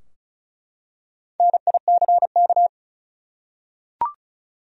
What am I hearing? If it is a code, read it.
Morse code, heard as a keyed tone, about 36 wpm, 700 Hz. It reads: NICK
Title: USA Common Names - Max 4 Letters - 35wpm